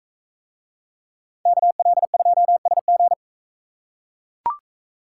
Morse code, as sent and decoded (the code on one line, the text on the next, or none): -.- .-.. ..--- ... --.
KL2SG